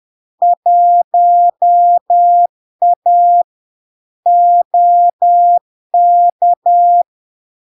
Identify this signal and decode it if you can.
Morse code, heard as a keyed tone, about 10 wpm, 700 Hz. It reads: 1A OK
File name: LARK-FD-2022-Log-in-CW-10wpm-700-Hz